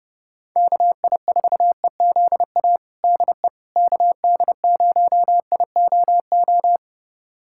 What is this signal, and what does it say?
Morse code, heard as a keyed tone, about 32 wpm, 700 Hz. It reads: KI4EZA DE KD0IOO
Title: USA Callsign DE Callsign - 6 Characters - 30wpm